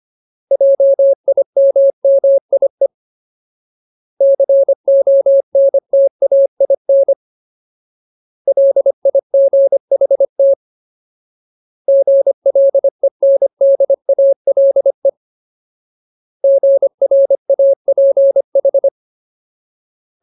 Morse code, heard as a keyed tone, about 25 wpm, 550 Hz. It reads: JIMMIE CONTAIN LIGHT GLENDALE GRAPH
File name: Ultimate Word List / 25WPM Ultimate